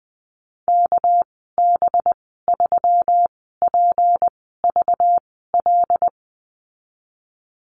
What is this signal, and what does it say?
Morse code, heard as a keyed tone, about 20 wpm, spaced as for 14 wpm, 700 Hz. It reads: KB3PVL
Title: 200 Lesson 32g - Letter k - Call Signs 20wpm